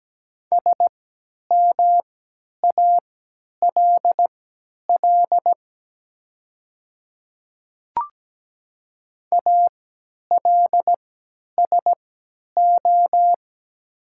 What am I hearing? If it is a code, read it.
Morse code, heard as a keyed tone, about 17 wpm, spaced as for 10 wpm, 700 Hz. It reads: SMALL ALSO
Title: 3x Character Spacing - Top 300 Words - 17wpm